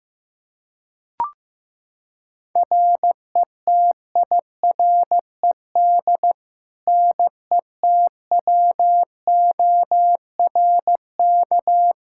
Morse code, heard as a keyed tone, about 15 wpm, 700 Hz. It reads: RETIRED NETWORK
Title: K1USN SST QSO Practice 15wpm